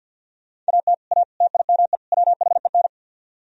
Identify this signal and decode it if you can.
Morse code, heard as a keyed tone, about 66 wpm, spaced as for 57 wpm, 700 Hz. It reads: AT A TIME WHEN
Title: Speed-Racing - Sets of 4 Words 50wpm